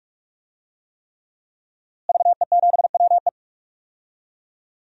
Morse code, heard as a keyed tone, about 43 wpm, 700 Hz. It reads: VE7WE
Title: DXCC-100 Call Signs - 5 Characters - Repeated 3x - 45wpm